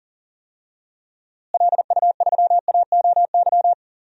Morse code, heard as a keyed tone, about 38 wpm, 700 Hz. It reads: LU3AOY
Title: DXCC-100 Call Signs - 6 Characters - Repeated 3x - 40wpm